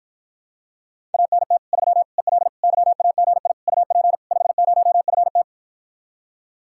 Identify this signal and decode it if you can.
Morse code, heard as a keyed tone, about 53 wpm, 700 Hz. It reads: ANT 3 EL YAGI UP 50FT